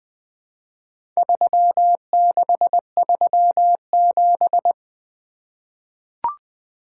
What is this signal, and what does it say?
Morse code, heard as a keyed tone, about 20 wpm, 700 Hz. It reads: 3637
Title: Four Letter-Number Combinations - 20wpm